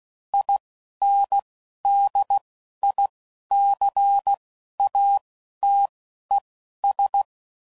Morse code, heard as a keyed tone, about 16 wpm, spaced as for 11 wpm, 800 Hz. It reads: INDICATES
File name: CW Code Groups @10WPM / Technical Subjects / XC